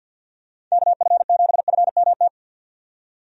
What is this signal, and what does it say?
Morse code, heard as a keyed tone, about 52 wpm, 700 Hz. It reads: KF7FMT
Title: USA Call Signs - 6 Characters - 50wpm